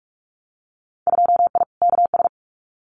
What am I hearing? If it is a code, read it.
Morse code, heard as a keyed tone, about 46 wpm, 700 Hz. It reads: 2I KS
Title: Winter Field Day Exchanges Encore 45wpm